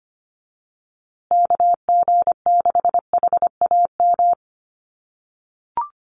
Morse code, heard as a keyed tone, about 25 wpm, 700 Hz. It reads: KG6HAM
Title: USA Call Signs Repeated 2x 25wpm